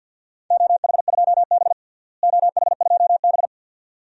Morse code, heard as a keyed tone, about 53 wpm, 700 Hz. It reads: OH2B OH2B
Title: DXCC-100 Call Signs - 4 Characters - Repeated 2x - 50wpm